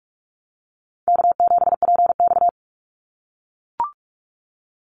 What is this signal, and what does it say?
Morse code, heard as a keyed tone, about 44 wpm, 700 Hz. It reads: K7PX
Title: NA SOTA Activator Callsigns 45wpm